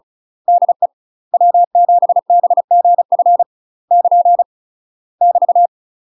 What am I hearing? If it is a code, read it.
Morse code, heard as a keyed tone, about 36 wpm, 700 Hz. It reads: DE W7BGF <KN> <BT>
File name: UK MASTER QSOs TRAINING / 35WPM / UK MASTER QSOs 0036